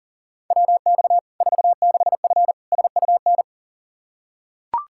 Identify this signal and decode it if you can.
Morse code, heard as a keyed tone, about 40 wpm, 700 Hz. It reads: WX 46F SUN
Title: Practice QSO - Exchange 2 - Info - 40wpm